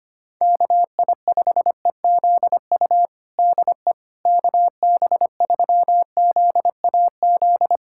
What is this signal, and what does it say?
Morse code, heard as a keyed tone, about 25 wpm, 700 Hz. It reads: KI5EZU DE KB3ZAZ
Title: USA Callsign DE Callsign - 6 Characters - 25wpm